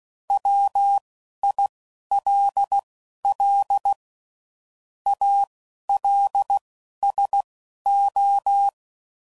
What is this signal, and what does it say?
Morse code, heard as a keyed tone, about 16 wpm, spaced as for 11 wpm, 800 Hz. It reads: WILL ALSO
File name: CW Code Groups @10WPM / Technical Subjects / XL